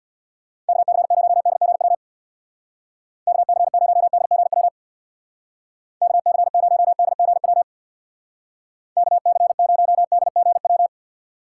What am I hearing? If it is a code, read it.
Morse code, heard as a keyed tone, about 64 wpm, 700 Hz. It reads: KC0DGW KC0DGW KC0DGW KC0DGW